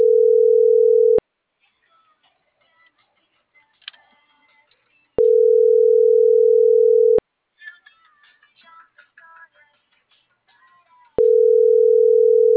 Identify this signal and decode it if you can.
Ringback tone (North America), 440 + 480 Hz.